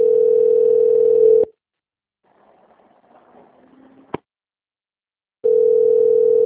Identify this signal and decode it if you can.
Ringback tone (North America), 440 + 480 Hz.
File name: rg-1600-unknown- - - .18627